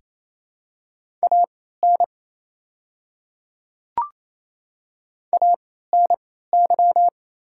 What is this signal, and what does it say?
Morse code, heard as a keyed tone, about 28 wpm, spaced as for 16 wpm, 690 Hz. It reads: AN ANY
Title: 3x Character Spacing - Top 200 Words - 28wpm